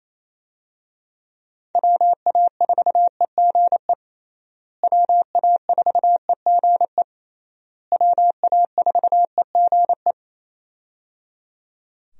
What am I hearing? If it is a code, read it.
Morse code, heard as a keyed tone, about 28 wpm, 700 Hz. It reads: WA4EGE WA4EGE WA4EGE
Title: USA Call Signs - 6 Characters - Repeated 3x - 28wpm